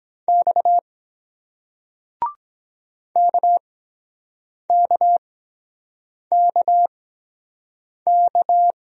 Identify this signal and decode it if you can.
Morse code, heard as a keyed tone, about 23 wpm, spaced as for 7 wpm, 700 Hz. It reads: X KKKK